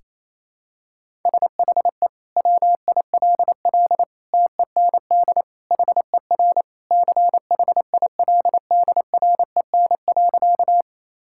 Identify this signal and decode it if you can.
Morse code, heard as a keyed tone, about 28 wpm, 700 Hz. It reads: SHE WILL TEND HER CHILDREN.
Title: Sentences from Top 1000 Words - Review Part 5 - 28wpm